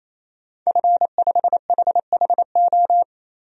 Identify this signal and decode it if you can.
Morse code, heard as a keyed tone, about 28 wpm, 700 Hz. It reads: F5HHO
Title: DXCC-100 Call Signs - 5 Characters - 28wpm